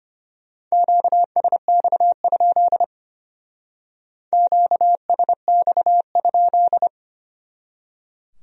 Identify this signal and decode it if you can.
Morse code, heard as a keyed tone, about 33 wpm, 700 Hz. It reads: QSX? QSX?